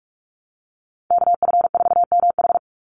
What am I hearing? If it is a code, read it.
Morse code, heard as a keyed tone, about 44 wpm, 700 Hz. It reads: KF4MH
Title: NA SOTA Chaser Callsigns 45wpm